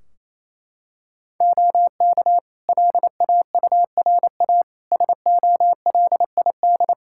Morse code, heard as a keyed tone, about 28 wpm, 700 Hz. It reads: OK LAURA SOLID
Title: K1USN SST QSO Practice 28wpm